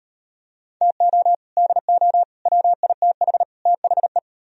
Morse code, heard as a keyed tone, about 35 wpm, 700 Hz. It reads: TO DO WITH THE